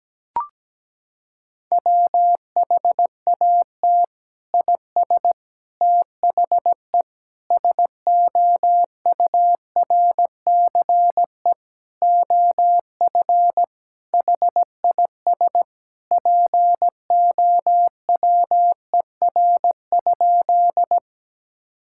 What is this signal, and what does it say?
Morse code, heard as a keyed tone, about 17 wpm, 700 Hz. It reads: WHAT IS THE SOURCE OF HIS POWER?
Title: Sentences from Top 1000 Words - Review Part 5 - 17wpm